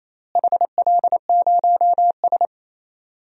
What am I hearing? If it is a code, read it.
Morse code, heard as a keyed tone, about 28 wpm, 700 Hz. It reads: HL0S
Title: DXCC-100 Call Signs - 4 Characters - Repeated 2x - 28wpm